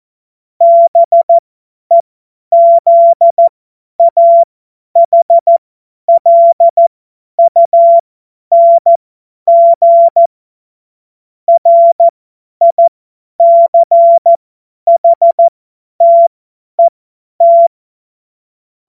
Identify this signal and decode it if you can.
Morse code, heard as a keyed tone, about 14 wpm, spaced as for 10 wpm, 680 Hz. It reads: BEZAHLUNG RICHTET
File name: Morsekurs / Witze / Witz 17. Januar 2023 50 F70